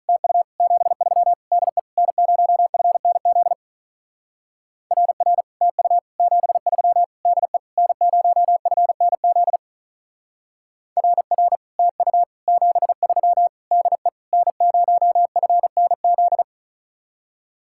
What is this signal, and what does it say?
Morse code, heard as a keyed tone, about 43 wpm, 700 Hz. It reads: TU 73 DE N0FNZ RR TU 73 DE N0FNZ RR TU 73 DE N0FNZ
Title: Speed-Racing Common QSO Phrases - 35wpm